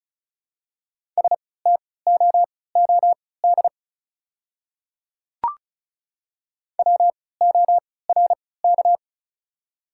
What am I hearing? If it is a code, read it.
Morse code, heard as a keyed tone, about 36 wpm, spaced as for 20 wpm, 700 Hz. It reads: STOOD WORK
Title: 3x Character Spacing - Top 500 Words - 35wpm